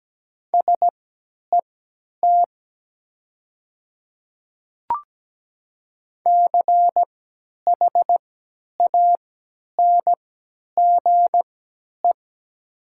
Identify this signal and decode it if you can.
Morse code, heard as a keyed tone, about 17 wpm, spaced as for 10 wpm, 700 Hz. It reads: SET CHANGE
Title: 3x Character Spacing - Top 200 Words - 17wpm